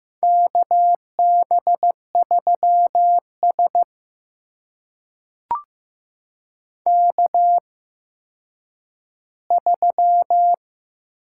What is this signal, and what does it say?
Morse code, heard as a keyed tone, about 15 wpm, spaced as for 9 wpm, 700 Hz. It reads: KB3S K3
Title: USA Call Signs - 4 Characters - 8x Character Spacing - 15wpm